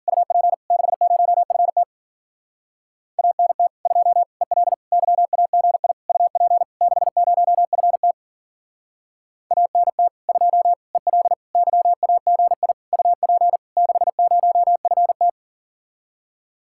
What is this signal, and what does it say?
Morse code, heard as a keyed tone, about 51 wpm, 700 Hz. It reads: UP 60FT ANT 2 EL YAGI UP 60FT ANT 2 EL YAGI UP 60FT